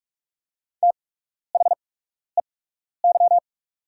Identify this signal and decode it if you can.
Morse code, heard as a keyed tone, about 47 wpm, spaced as for 12 wpm, 690 Hz. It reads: THEY
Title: 8x Character Spacing - Top 200 Words - 45wpm